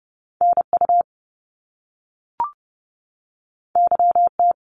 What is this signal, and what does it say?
Morse code, heard as a keyed tone, about 30 wpm, 700 Hz. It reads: NU YT
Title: Canadian Provinces and Territories 30wpm